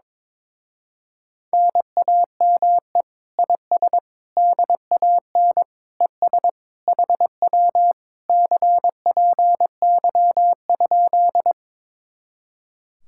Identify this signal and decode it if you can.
Morse code, heard as a keyed tone, about 22 wpm, 700 Hz. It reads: NAME IS DAN ES HW CPY?
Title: Speed-Racing Common QSO Phrases - 22wpm